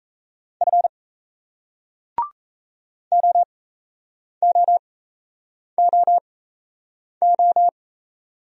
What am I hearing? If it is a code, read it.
Morse code, heard as a keyed tone, about 38 wpm, spaced as for 8 wpm, 700 Hz. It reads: F OOOO